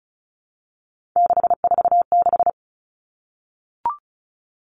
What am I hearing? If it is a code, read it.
Morse code, heard as a keyed tone, about 36 wpm, 700 Hz. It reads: 646
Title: Three Challenging Letter-Number Combinations 35wpm